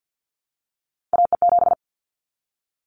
Morse code, heard as a keyed tone, about 50 wpm, 700 Hz. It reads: AE7
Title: Callsign Prefixes for Top 25 DXCC - Repeated 3x - 50wpm